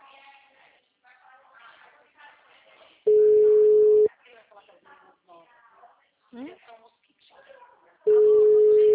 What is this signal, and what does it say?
Ringback tone (Europe, 425 Hz), with speech.
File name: out- -2012- - - .2206